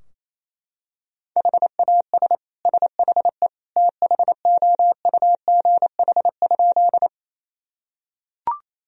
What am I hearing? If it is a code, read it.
Morse code, heard as a keyed tone, about 27 wpm, 700 Hz. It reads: HAS SHE THOUGH?
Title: Sentences from Top 1000 Words - Review Part 2 - 28wpm